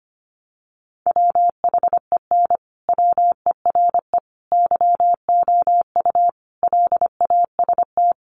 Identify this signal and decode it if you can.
Morse code, heard as a keyed tone, about 25 wpm, 700 Hz. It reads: WHEN WERE YOU LAST
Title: Sentences from Top 1300 Words 25wpm